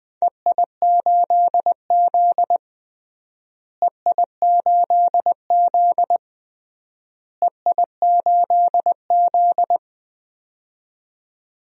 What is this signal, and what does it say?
Morse code, heard as a keyed tone, about 21 wpm, 700 Hz. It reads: EI8Z EI8Z EI8Z
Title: DXCC-100 Call Signs - 4 Characters - Repeated 3x - 20wpm